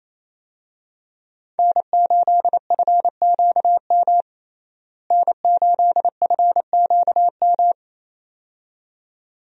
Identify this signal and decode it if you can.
Morse code, heard as a keyed tone, about 28 wpm, 700 Hz. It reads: N8FQM N8FQM
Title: USA Call Signs - 5 Characters - Repeated 2x - 28wpm